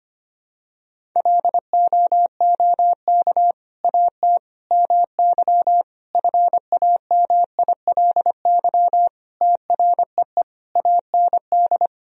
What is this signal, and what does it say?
Morse code, heard as a keyed tone, about 25 wpm, 700 Hz. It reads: LOOK AT MY FAMILY TREE AND
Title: Sentences from Top 1000 Words - Review Part 1 - 25wpm